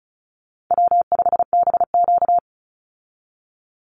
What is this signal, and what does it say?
Morse code, heard as a keyed tone, about 34 wpm, 700 Hz. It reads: W5BQ
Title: Call Signs Repeated 3x - 35wpm